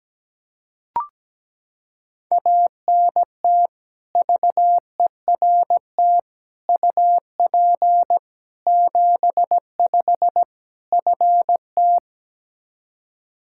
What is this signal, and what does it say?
Morse code, heard as a keyed tone, about 17 wpm, 700 Hz. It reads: ANT VERT UP 75 FT
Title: K1USN SST QSO Practice 17wpm